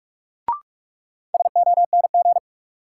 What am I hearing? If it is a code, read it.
Morse code, heard as a keyed tone, about 46 wpm, 700 Hz. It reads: SONG
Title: Sentences from Top 400 Words - New Word Drill - 45wpm